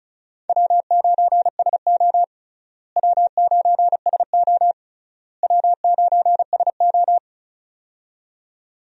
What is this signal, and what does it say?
Morse code, heard as a keyed tone, about 36 wpm, 700 Hz. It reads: W9SO W9SO W9SO
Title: USA Call Signs - 4 Characters - Repeated 3x - 35wpm